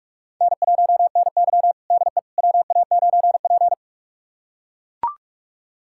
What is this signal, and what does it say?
Morse code, heard as a keyed tone, about 41 wpm, 700 Hz. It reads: N1NY DE WA9P
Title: USA Callsign DE Callsign - 4 Characters - 45wpm